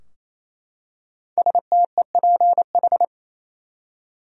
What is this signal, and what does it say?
Morse code, heard as a keyed tone, about 28 wpm, 700 Hz. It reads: STEPH
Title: USA Common Names - 28wpm